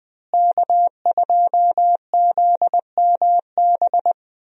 Morse code, heard as a keyed tone, about 20 wpm, 700 Hz. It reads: K2ZMB
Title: USA Call Signs - 5 Characters - Repeated 3x - 20wpm